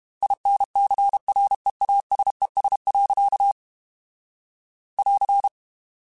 Morse code, heard as a keyed tone, about 32 wpm, 800 Hz. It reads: INCREASES. <AR>
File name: Technical Subjects / Technical Subjects-mp3s / XC @30